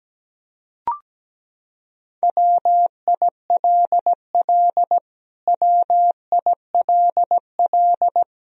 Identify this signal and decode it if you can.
Morse code, heard as a keyed tone, about 17 wpm, 700 Hz. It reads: WILL WILL
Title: Top 100 Words Repeated 17wpm